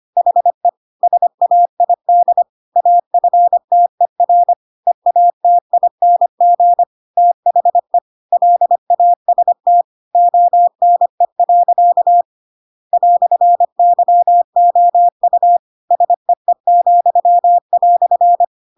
Morse code, heard as a keyed tone, about 25 wpm, 700 Hz. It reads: HE SAID AFTER EATING THE LAST ONE. "YOU SEE,"
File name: Pinocchio / Pinocchio ch7 25 25 ik1hge